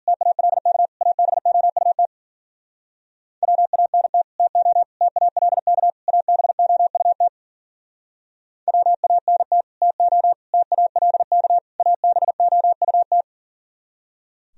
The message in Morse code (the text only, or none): TALK ABOUT WANT TO TALK ABOUT WANT TO TALK ABOUT